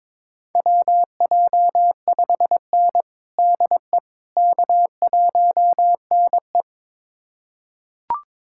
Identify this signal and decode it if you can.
Morse code, heard as a keyed tone, about 23 wpm, 700 Hz. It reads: WJ5N DE K1NE
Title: USA Callsign DE Callsign - 4 Characters - 22wpm